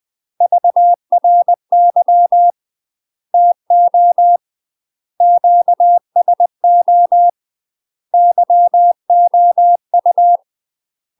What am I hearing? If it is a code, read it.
Morse code, heard as a keyed tone, about 20 wpm, 700 Hz. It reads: VRY TO QSO YOU
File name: UK MASTER QSOs TRAINING/20WPM